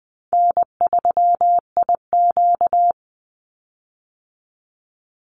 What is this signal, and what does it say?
Morse code, heard as a keyed tone, about 20 wpm, 700 Hz. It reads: N3IQ
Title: Call Signs Repeated 2x - 20wpm